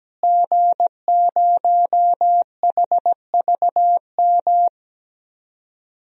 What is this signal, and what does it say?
Morse code, heard as a keyed tone, about 17 wpm, 700 Hz. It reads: G0HVM
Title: DXCC-100 Call Signs - 5 Characters - Repeated 3x - 17wpm